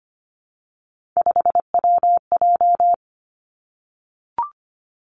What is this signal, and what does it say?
Morse code, heard as a keyed tone, about 25 wpm, 700 Hz. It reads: 5WJ
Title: Three Challenging Letter-Number Combinations 25wpm